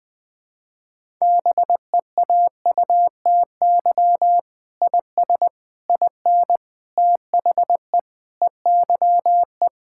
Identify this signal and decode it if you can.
Morse code, heard as a keyed tone, about 20 wpm, 700 Hz. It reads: BEAUTY IS IN THE EYE
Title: Sentences with 8-Word Phrases - Spoken Beginning - 20wpm